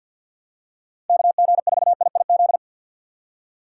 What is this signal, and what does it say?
Morse code, heard as a keyed tone, about 52 wpm, 700 Hz. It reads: KG4IIZ
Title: USA Call Signs - 6 Characters - 6x Character Spacing - 50wpm